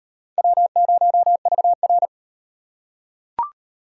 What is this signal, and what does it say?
Morse code, heard as a keyed tone, about 38 wpm, 700 Hz. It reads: W0VR